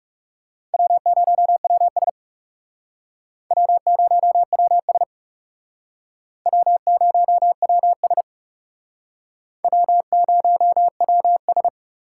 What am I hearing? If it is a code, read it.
Morse code, heard as a keyed tone, about 37 wpm, 700 Hz. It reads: W0WS W0WS W0WS W0WS
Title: Speed Racing - USA 1x2 Call Signs 30wpm